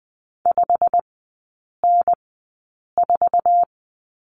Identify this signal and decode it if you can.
Morse code, heard as a keyed tone, about 20 wpm, spaced as for 8 wpm, 700 Hz. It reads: 5N4
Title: 205 Lesson 33e - Letter j - Three Characters 20wpm